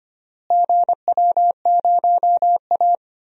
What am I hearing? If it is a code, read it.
Morse code, heard as a keyed tone, about 25 wpm, 700 Hz. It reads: GW0A
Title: DXCC-100 Call Signs - 4 Characters - Repeated 2x - 25wpm